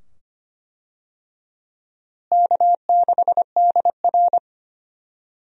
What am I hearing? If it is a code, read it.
Morse code, heard as a keyed tone, about 25 wpm, 700 Hz. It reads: K6DR
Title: USA Call Signs - 4 Characters - Repeated 2x - 25wpm